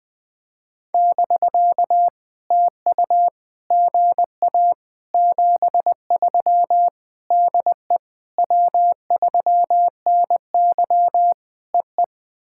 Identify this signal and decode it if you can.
Morse code, heard as a keyed tone, about 19 wpm, 700 Hz. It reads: <BK> TU GA 73 DE W3NY EE
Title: POTA QSO Practice - 20wpm